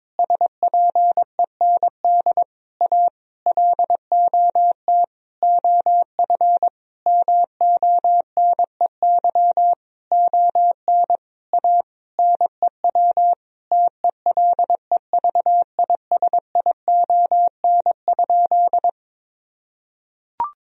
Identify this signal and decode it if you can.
Morse code, heard as a keyed tone, about 22 wpm, 700 Hz. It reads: SPEND A LOT OF MONEY ON A NEW TELEVISION?
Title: Sentences from Top 1000 Words - Review Part 4 - 22wpm